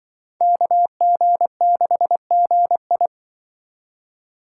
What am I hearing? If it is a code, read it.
Morse code, heard as a keyed tone, about 24 wpm, 700 Hz. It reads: KG6GI
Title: Speed Racing - USA 2x2 Call Signs 20wpm